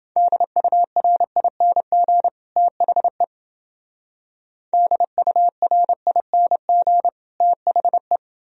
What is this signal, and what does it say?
Morse code, heard as a keyed tone, about 29 wpm, 700 Hz. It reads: DURING THE DURING THE